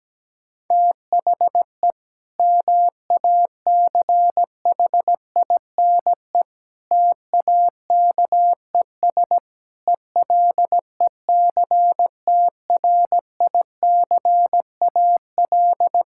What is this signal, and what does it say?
Morse code, heard as a keyed tone, about 17 wpm, 700 Hz. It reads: THE MACHINE TAKES ELECTRICAL
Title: Sentences from Top 1000 Words - Review Part 6 - 17wpm